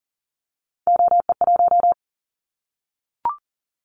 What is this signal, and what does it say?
Morse code, heard as a keyed tone, about 40 wpm, 690 Hz. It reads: OE1
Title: Callsign Prefixes for Top 100 DXCC - 40wpm